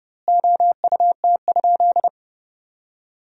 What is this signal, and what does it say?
Morse code, heard as a keyed tone, about 30 wpm, 700 Hz. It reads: OUT?
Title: 2x Word-Spacing - Sentences from Top 300 Words - 30wpm